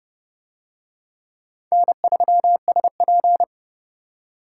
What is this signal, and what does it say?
Morse code, heard as a keyed tone, about 29 wpm, 700 Hz. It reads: N3SP